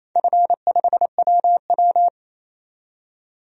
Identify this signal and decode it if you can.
Morse code, heard as a keyed tone, about 28 wpm, 700 Hz. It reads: F5WW
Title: DXCC-100 Call Signs - 4 Characters - 28wpm